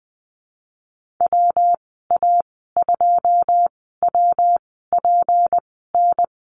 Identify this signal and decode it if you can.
Morse code, heard as a keyed tone, about 20 wpm, spaced as for 15 wpm, 700 Hz. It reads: WA2WPN
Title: 161 Lesson 26g - Letter g - Call Signs 20wpm